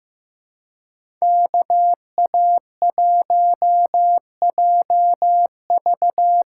Morse code, heard as a keyed tone, about 15 wpm, 700 Hz. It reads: KA1JV
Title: USA Call Signs - 6 Characters - 6x Character Spacing - 15wpm